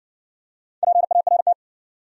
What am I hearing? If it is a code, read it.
Morse code, heard as a keyed tone, about 57 wpm, 700 Hz. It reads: PART